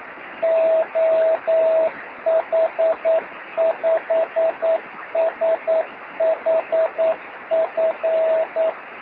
Morse code, heard as a keyed tone, about 9 wpm, 650 Hz, with radio noise. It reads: OH5SHF
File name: sm7lcb / oh5shf 13cm